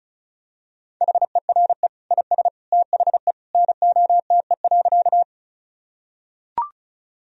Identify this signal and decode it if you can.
Morse code, heard as a keyed tone, about 36 wpm, 700 Hz. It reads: HERE IS THE NOTE.
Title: Sentences from Top 1000 Words - Review Part 2 - 35wpm